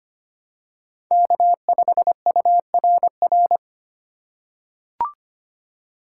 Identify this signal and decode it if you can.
Morse code, heard as a keyed tone, about 25 wpm, 700 Hz. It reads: K5URR
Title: USA Call Signs - 8x Character Spacing - 25wpm